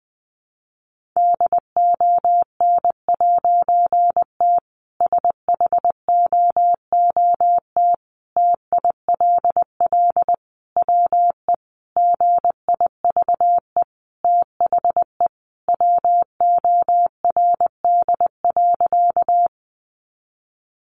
Morse code, heard as a keyed tone, about 20 wpm, 700 Hz. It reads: DON'T SHOOT TILL WE GIVE THE WORD.
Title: A Sound Of Thunder 20wpm